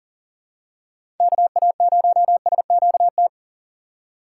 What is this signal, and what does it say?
Morse code, heard as a keyed tone, about 40 wpm, 700 Hz. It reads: KA0SQT
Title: USA Call Signs - 6 Characters - 6x Character Spacing - 40wpm